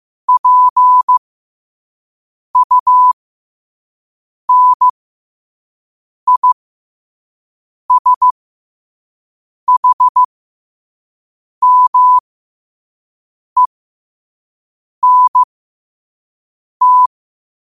Morse code, heard as a keyed tone, about 15 wpm, spaced as for 5 wpm, 1000 Hz. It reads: PUNISHMENT
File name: qotdfiles / 5wpm